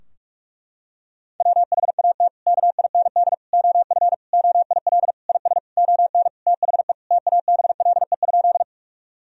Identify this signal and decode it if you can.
Morse code, heard as a keyed tone, about 46 wpm, 700 Hz. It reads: WHAT KIND OF OIL IS ON THE TABLE?
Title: Sentences from Top 1000 Words - Review Part 4 - 45wpm